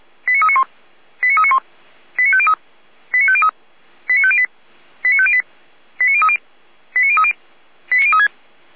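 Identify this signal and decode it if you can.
Five-tone selective call (ZVEI), heard as quick runs of five tones, tones of about 70 ms. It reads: 89391 89391 89503 89503 89508 89508 89030 89030 89036
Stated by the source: ZVEI five-tone selective call received on SDR.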